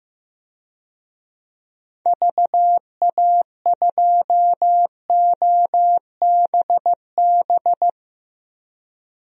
Morse code, heard as a keyed tone, about 15 wpm, 700 Hz. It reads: VA2OBB
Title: DXCC-100 Call Signs - 6 Characters - Repeated 3x - 15wpm